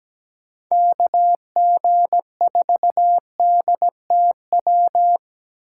Morse code, heard as a keyed tone, about 17 wpm, 700 Hz. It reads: KG4DTW
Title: USA Call Signs - 6 Characters - 2x Character Spacing - 17wpm